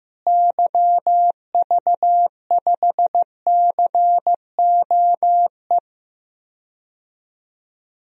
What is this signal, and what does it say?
Morse code, heard as a keyed tone, about 15 wpm, 700 Hz. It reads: YV5COE
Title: DXCC-100 Call Signs - 6 Characters - Repeated 3x - 15wpm